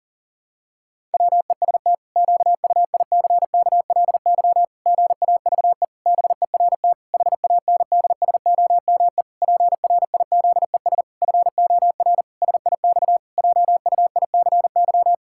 Morse code, heard as a keyed tone, about 40 wpm, 700 Hz. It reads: WEST QUICKLY GAVE BERT HANDSOME PRIZES FOR SIX JUICY
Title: Pangram Challenge v2 - 40wpm